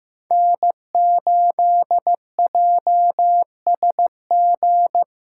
Morse code, heard as a keyed tone, about 15 wpm, 700 Hz. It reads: N8JSG
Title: USA Call Signs - 5 Characters - 2x Character Spacing - 15wpm